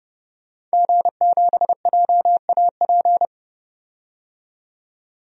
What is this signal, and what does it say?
Morse code, heard as a keyed tone, about 30 wpm, 690 Hz. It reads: G7JAP
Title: DXCC-100 Call Signs - 5 Characters - 30wpm